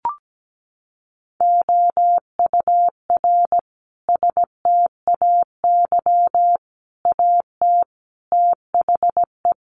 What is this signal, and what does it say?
Morse code, heard as a keyed tone, about 17 wpm, 700 Hz. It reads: OUR STAY AT THE
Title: Sentences from Top 1400 Words 17wpm